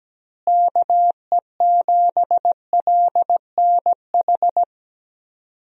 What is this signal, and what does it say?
Morse code, heard as a keyed tone, about 17 wpm, 700 Hz. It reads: KE7LNH
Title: USA Call Signs - 6 Characters - Repeated 3x - 17wpm